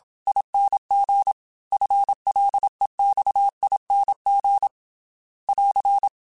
Morse code, heard as a keyed tone, about 27 wpm, 800 Hz. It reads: ING FLEXING <AR>
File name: FL / FL mp3s / Fast FL @25